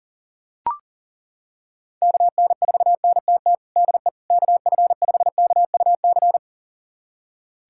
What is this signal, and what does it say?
Morse code, heard as a keyed tone, about 40 wpm, 700 Hz. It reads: KN4NTT DE KF5KUC
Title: USA Callsign DE Callsign 40wpm